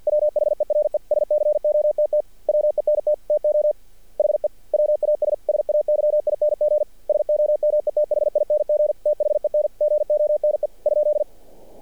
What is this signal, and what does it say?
Morse code, heard as a keyed tone, about 50 wpm, 600 Hz. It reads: WHERE SCOTT WENT TO HE WAS SAYING SOMETHING THEN GONE ?
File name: archived / CRSnet